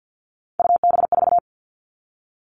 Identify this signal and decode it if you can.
Morse code, heard as a keyed tone, about 50 wpm, 700 Hz. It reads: UB4
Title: Three Challenging Letter-Number Combinations 50wpm